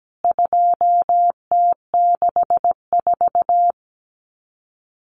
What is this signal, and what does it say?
Morse code, heard as a keyed tone, about 17 wpm, 700 Hz. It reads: 2T64
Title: CWT Exchange 17wpm